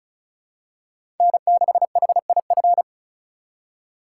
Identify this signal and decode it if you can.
Morse code, heard as a keyed tone, about 36 wpm, 700 Hz. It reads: N6HIF
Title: USA Call Signs - 5 Characters - 6x Character Spacing - 35wpm